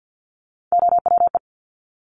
Morse code, heard as a keyed tone, about 50 wpm, 700 Hz. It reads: OWE
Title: Top Three Letter Combinations in Words 50wpm